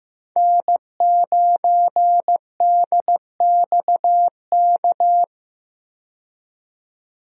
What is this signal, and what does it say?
Morse code, heard as a keyed tone, about 15 wpm, 700 Hz. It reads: N9DXK
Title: USA Call Signs - 5 Characters - 15wpm